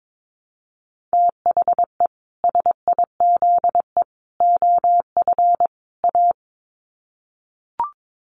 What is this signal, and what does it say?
Morse code, heard as a keyed tone, about 22 wpm, 700 Hz. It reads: THE SIZE OF A
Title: Sets of 4 Words 22wpm